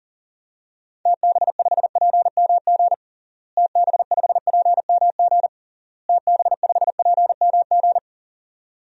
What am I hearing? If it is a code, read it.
Morse code, heard as a keyed tone, about 40 wpm, 700 Hz. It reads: TB5PMG TB5PMG TB5PMG
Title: DXCC-100 Call Signs - 6 Characters - Repeated 3x - 40wpm